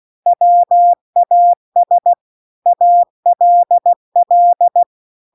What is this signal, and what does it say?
Morse code, heard as a keyed tone, about 16 wpm, 700 Hz. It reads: WAS ALL
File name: CW-The-Big-Twoheaded-River-2-16WPM